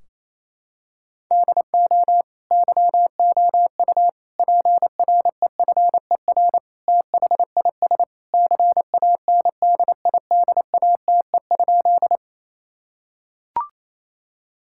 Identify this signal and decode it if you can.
Morse code, heard as a keyed tone, about 28 wpm, 700 Hz. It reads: DO YOU PREFER THIS CANDIDATE?
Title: Sentences from Top 1600 Words 28wpm